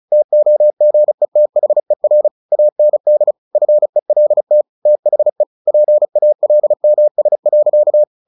Morse code, heard as a keyed tone, about 34 wpm, 600 Hz. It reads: TOGETHER AND FELT THE PALMS.
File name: the old man and the sea 2 35wpm